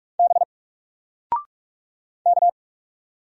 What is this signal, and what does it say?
Morse code, heard as a keyed tone, about 44 wpm, 700 Hz. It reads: B K
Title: Single Letter-Number 45wpm